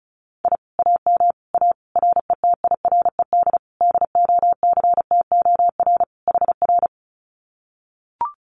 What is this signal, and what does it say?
Morse code, heard as a keyed tone, about 34 wpm, 700 Hz. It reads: I AM A RETIRED DOCTOR HR
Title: Common QSO Phrases Encore 35wpm